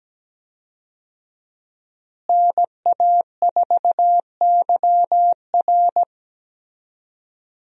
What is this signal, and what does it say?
Morse code, heard as a keyed tone, about 17 wpm, 700 Hz. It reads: NA4YR
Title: USA Call Signs - 5 Characters - Repeated 3x - 17wpm